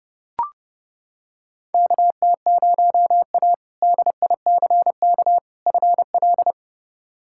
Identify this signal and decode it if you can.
Morse code, heard as a keyed tone, about 30 wpm, 700 Hz. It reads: KT0A DICK FL
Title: K1USN SST QSO Practice - 30wpm